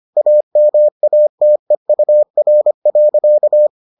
Morse code, heard as a keyed tone, about 25 wpm, 600 Hz. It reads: AMATEUR.